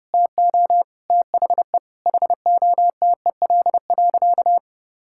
Morse code, heard as a keyed tone, about 30 wpm, 700 Hz. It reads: TO THE HOTEL.
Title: Sentences from Top 1600 Words 30wpm